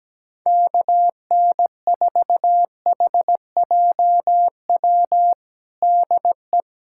Morse code, heard as a keyed tone, about 17 wpm, 700 Hz. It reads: KN4HJW DE
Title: USA Callsign DE Callsign - 6 Characters - 17wpm